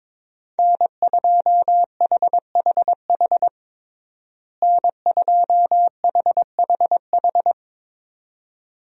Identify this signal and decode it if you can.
Morse code, heard as a keyed tone, about 22 wpm, 700 Hz. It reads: N2HHH N2HHH
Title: USA Call Signs - 5 Characters - Repeated 2x - 22wpm